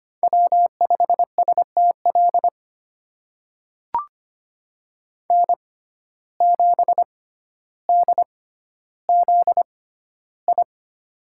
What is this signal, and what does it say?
Morse code, heard as a keyed tone, about 25 wpm, spaced as for 13 wpm, 700 Hz. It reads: W5STL N7DZI
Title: USA Call Signs - 5 Characters - 6x Character Spacing - 25wpm